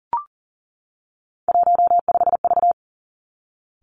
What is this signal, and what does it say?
Morse code, heard as a keyed tone, about 41 wpm, 700 Hz. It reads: 15V
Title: Three Challenging Letter-Number Combinations 40wpm